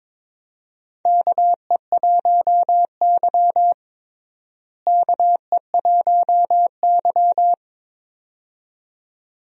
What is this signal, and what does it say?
Morse code, heard as a keyed tone, about 22 wpm, 700 Hz. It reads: KE1Y KE1Y
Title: USA Call Signs - 4 Characters - Repeated 2x - 22wpm